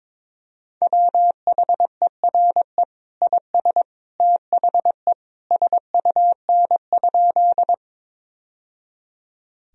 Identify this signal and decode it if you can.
Morse code, heard as a keyed tone, about 22 wpm, 700 Hz. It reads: WHERE IS THE SUN?
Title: Sentences from Top 1000 Words - Review Part 1 - 22wpm